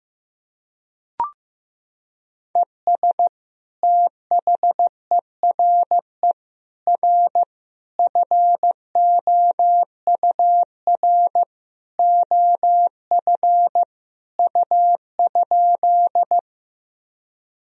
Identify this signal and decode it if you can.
Morse code, heard as a keyed tone, about 15 wpm, 700 Hz. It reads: ES THERE R FOUR OF U?
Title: Sentences from Top 300 Words and Abbreviations 15wpm